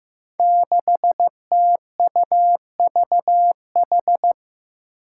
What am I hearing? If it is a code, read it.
Morse code, heard as a keyed tone, about 15 wpm, 700 Hz. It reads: 6TUVH
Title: Five Challenging Letter-Number Combinations 15wpm